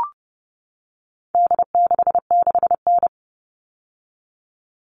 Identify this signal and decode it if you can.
Morse code, heard as a keyed tone, about 30 wpm, 700 Hz. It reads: D66N
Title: All Cut Numbers - 4-Digit 30wpm